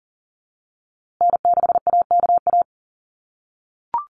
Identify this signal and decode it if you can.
Morse code, heard as a keyed tone, about 40 wpm, 700 Hz. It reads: N6AKA
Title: USA Call Signs Repeated 2x 40wpm